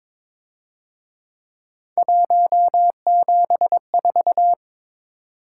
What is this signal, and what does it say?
Morse code, heard as a keyed tone, about 26 wpm, 700 Hz. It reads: 174
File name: Speed-Racing - Three Letter-Number Combinations - 22wpm